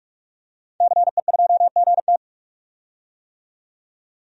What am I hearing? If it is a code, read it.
Morse code, heard as a keyed tone, about 46 wpm, 700 Hz. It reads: CE2GT
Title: DXCC-100 Call Signs - 5 Characters - 45wpm